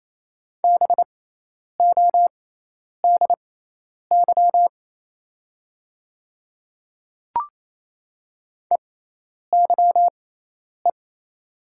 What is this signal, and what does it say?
Morse code, heard as a keyed tone, about 27 wpm, spaced as for 10 wpm, 700 Hz. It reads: BODY EYE
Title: 6x Character Spacing - Top 500 Words - 28wpm